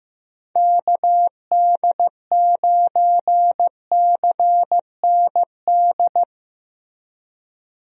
Morse code, heard as a keyed tone, about 15 wpm, 700 Hz. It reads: KD9CND
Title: USA Call Signs - 6 Characters - Repeated 2x - 15wpm